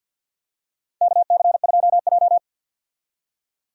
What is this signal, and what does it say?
Morse code, heard as a keyed tone, about 51 wpm, 700 Hz. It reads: KC2J
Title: USA Call Signs - 4 Characters - 50wpm